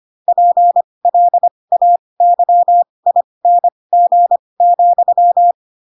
Morse code, heard as a keyed tone, about 25 wpm, spaced as for 20 wpm, 700 Hz. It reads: PLAYING,
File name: Pinocchio / Pinocchio ch4 15 25 ik1hge